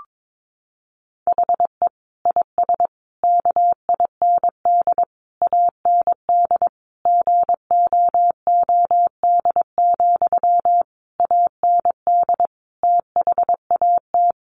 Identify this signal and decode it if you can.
Morse code, heard as a keyed tone, about 22 wpm, 700 Hz. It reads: HE IS KIND AND GOOD, AND THAT
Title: Sentences from Top 1400 Words 22wpm